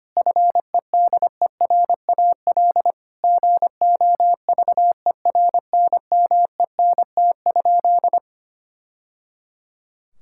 Morse code, heard as a keyed tone, about 25 wpm, 700 Hz. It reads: FEDERAL GOVERNMENT?
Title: Sentences from Top 1000 Words - Review Part 5 - 25wpm